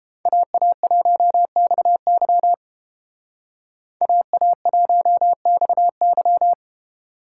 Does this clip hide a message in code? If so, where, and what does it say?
Yes: Morse code from 0.3 to 6.5 s, heard as a keyed tone, about 32 wpm, 700 Hz. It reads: AA1XY AA1XY